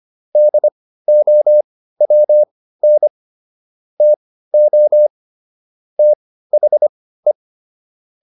Morse code, heard as a keyed tone, about 25 wpm, spaced as for 15 wpm, 600 Hz. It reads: DOWN TO THE
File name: SS119 15